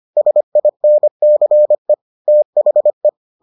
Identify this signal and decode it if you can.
Morse code, heard as a keyed tone, about 25 wpm, 600 Hz. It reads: SINCE THE